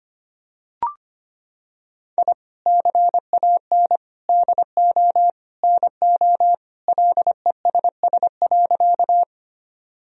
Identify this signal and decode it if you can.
Morse code, heard as a keyed tone, about 25 wpm, 700 Hz. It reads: I CAN DO NO LESS.
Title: Sentences from Top 1000 Words - Review Part 2 - 25wpm